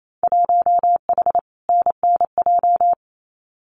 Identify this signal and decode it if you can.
Morse code, heard as a keyed tone, about 28 wpm, 700 Hz. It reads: 1H NNJ
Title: Winter Field Day Exchanges 28wpm